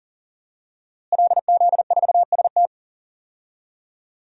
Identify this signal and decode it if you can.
Morse code, heard as a keyed tone, about 37 wpm, 700 Hz. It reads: LZ4ST
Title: DXCC-100 Call Signs - 5 Characters - Repeated 2x - 40wpm